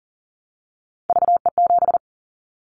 Morse code, heard as a keyed tone, about 40 wpm, 700 Hz. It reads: VE7
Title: Callsign Prefixes for Top 25 DXCC - Repeated 3x - 40wpm